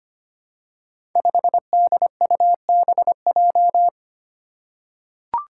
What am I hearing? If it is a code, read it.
Morse code, heard as a keyed tone, about 25 wpm, 700 Hz. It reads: 5DUBJ
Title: Five Challenging Letter-Number Combinations 25wpm